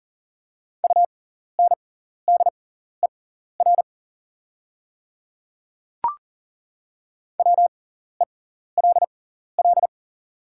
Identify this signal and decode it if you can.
Morse code, heard as a keyed tone, about 40 wpm, spaced as for 14 wpm, 690 Hz. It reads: UNDER WELL
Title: 6x Character Spacing - Top 200 Words - 40wpm